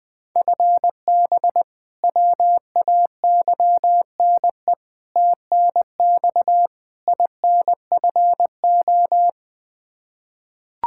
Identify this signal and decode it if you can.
Morse code, heard as a keyed tone, about 21 wpm, 700 Hz. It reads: FB WAYNE TNX INFO
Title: K1USN SST QSO Practice 20wpm